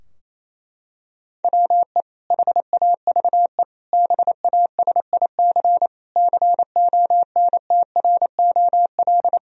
Morse code, heard as a keyed tone, about 28 wpm, 700 Hz. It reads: WE HAVE BASIC CONTROL
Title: Sentences from Top 1000 Words - Review Part 3 - 28wpm